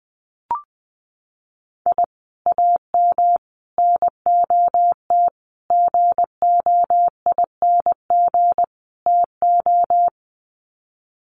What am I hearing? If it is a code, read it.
Morse code, heard as a keyed tone, about 20 wpm, 700 Hz. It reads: I AM NOT GOING TO
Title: Sets of 5 Words Encore 20wpm